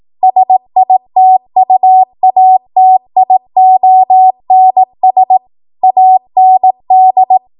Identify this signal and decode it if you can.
Morse code, heard as a keyed tone, about 18 wpm, 750 Hz. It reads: SITUATIONS AND
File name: Morse / 18WPM